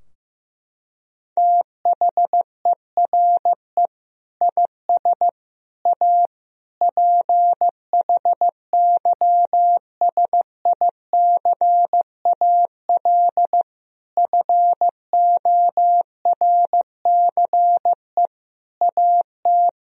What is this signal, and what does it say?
Morse code, heard as a keyed tone, about 16 wpm, 700 Hz. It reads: THERE IS A PHYSICAL FORCE AT
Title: Sentences from Top 1000 Words - Review Part 4 - 15wpm